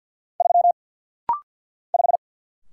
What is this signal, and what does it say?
Morse code, heard as a keyed tone, about 50 wpm, 700 Hz. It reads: 3 5
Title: Single Letter-Number - Rapid-Fire 50wpm